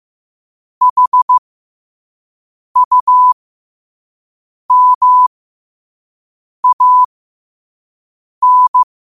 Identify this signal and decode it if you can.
Morse code, heard as a keyed tone, about 15 wpm, spaced as for 5 wpm, 1000 Hz. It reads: HUMAN
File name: qotdfiles / 5wpm